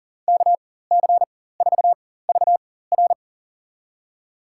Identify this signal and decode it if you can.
Morse code, heard as a keyed tone, about 37 wpm, spaced as for 18 wpm, 700 Hz. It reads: KC4VR
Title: USA Call Signs - 5 Characters - 4x Character Spacing - 40wpm